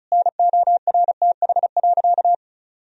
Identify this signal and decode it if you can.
Morse code, heard as a keyed tone, about 34 wpm, 700 Hz. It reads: NORTH.
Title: Sentences from Top 1000 Words - Review Part 1 - 35wpm